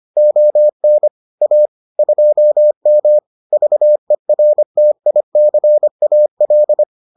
Morse code, heard as a keyed tone, about 25 wpm, 600 Hz. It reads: ON A 2M VERTICAL